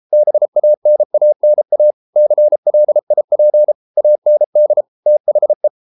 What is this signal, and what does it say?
Morse code, heard as a keyed tone, about 33 wpm, 600 Hz. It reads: BANANA CLIP AND THE